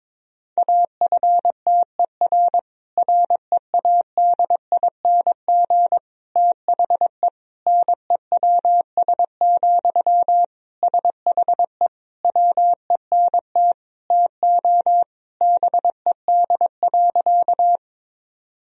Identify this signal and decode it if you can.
Morse code, heard as a keyed tone, about 22 wpm, 700 Hz. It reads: AFTER READING THE NEWS, SHE WENT TO BED.
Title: Sentences from Top 1000 Words - Review Part 3 - 22wpm